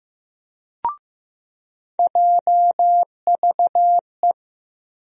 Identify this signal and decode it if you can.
Morse code, heard as a keyed tone, about 15 wpm, 700 Hz. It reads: JVE
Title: USA Call Sign Postfixes - 3 Letters 15wpm